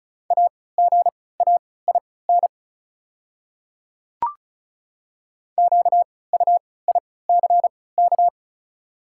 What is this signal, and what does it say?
Morse code, heard as a keyed tone, about 36 wpm, spaced as for 20 wpm, 690 Hz. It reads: AGAIN QUICK
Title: 3x Character Spacing - Top 500 Words - 35wpm